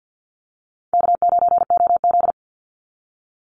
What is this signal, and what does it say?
Morse code, heard as a keyed tone, about 49 wpm, 700 Hz. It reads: K9OZ
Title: NA SOTA Chaser Callsigns 50wpm